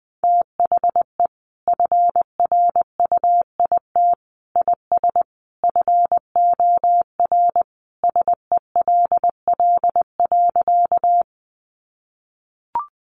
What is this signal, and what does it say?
Morse code, heard as a keyed tone, about 20 wpm, 700 Hz. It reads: THE FRUIT IS FOR SELL.
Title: Sentences from Top 1300 Words 20wpm